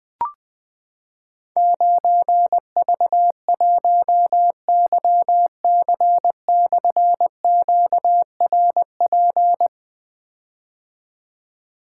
Morse code, heard as a keyed tone, about 21 wpm, 700 Hz. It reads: 9V1YC/QRP
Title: Difficult Call Signs 20wpm